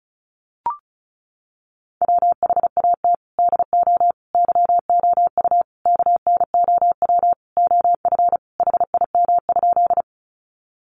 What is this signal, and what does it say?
Morse code, heard as a keyed tone, about 34 wpm, 700 Hz. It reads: WHAT DO YOU KNOW OF HIM?
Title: Sentences from Top 100 Words 35wpm